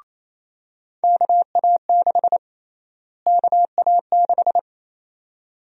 Morse code, heard as a keyed tone, about 28 wpm, 700 Hz. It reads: KA6 KA6
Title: USA Call Sign Prefixes - Repeated 2x - 28wpm